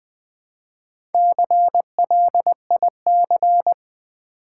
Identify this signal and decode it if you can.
Morse code, heard as a keyed tone, about 20 wpm, 700 Hz. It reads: CLIC
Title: Build A Word - 1 Letter At A Time - 3 to 5-Letter Words - 20wpm